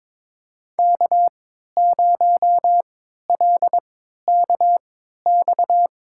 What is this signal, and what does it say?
Morse code, heard as a keyed tone, about 22 wpm, spaced as for 12 wpm, 700 Hz. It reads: K0LKX
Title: USA Call Signs - 5 Characters - 3x Character Spacing - 22wpm